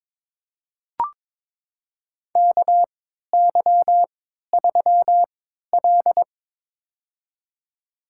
Morse code, heard as a keyed tone, about 21 wpm, spaced as for 12 wpm, 700 Hz. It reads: KY3L
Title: USA Call Signs - 4 Characters - 3x Character Spacing - 22wpm